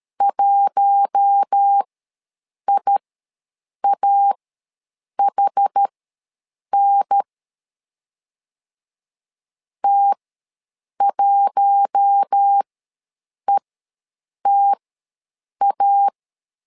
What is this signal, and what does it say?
Morse code, heard as a keyed tone, about 13 wpm, spaced as for 7 wpm, 800 Hz. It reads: 1IAHN T1ETA
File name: LEZIONI / 4 Q1 ws 12wpm 2129